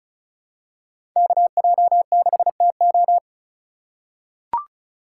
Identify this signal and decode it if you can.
Morse code, heard as a keyed tone, about 34 wpm, 700 Hz. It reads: KJ6TO
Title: USA Call Signs - 5 Characters - 3x Character Spacing - 35wpm